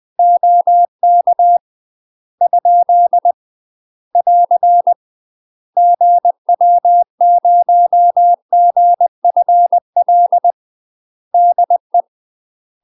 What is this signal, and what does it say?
Morse code, heard as a keyed tone, about 20 wpm, 700 Hz. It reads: OK ? <AR> GW0GFL DE
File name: UK MASTER QSOs TRAINING / 20WPM / UK MASTER QSOs 0013